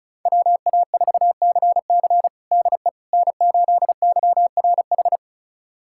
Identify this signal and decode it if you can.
Morse code, heard as a keyed tone, about 34 wpm, 700 Hz. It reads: WA4CC DE N8YRH
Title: USA Callsign DE Callsign - 5 Characters - 35wpm